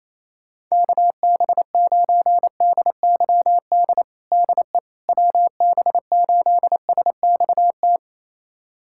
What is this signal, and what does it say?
Morse code, heard as a keyed tone, about 28 wpm, 700 Hz. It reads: KB9DYD DE WB8SXT
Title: USA Callsign DE Callsign - 6 Characters - 28wpm